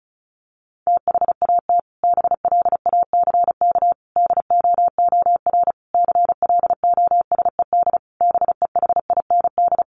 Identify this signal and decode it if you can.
Morse code, heard as a keyed tone, about 34 wpm, 700 Hz. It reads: THAT BLACK DOOR CLOSED BEHIND
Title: Sentences from Top 1400 Words 35wpm